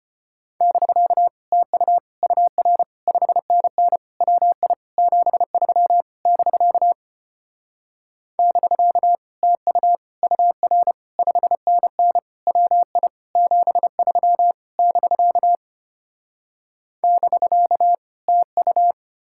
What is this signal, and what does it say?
Morse code, heard as a keyed tone, about 33 wpm, 700 Hz. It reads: <BK> TU UR 5NN WI 73 <BK> <BK> TU UR 5NN WI 73 <BK> <BK> TU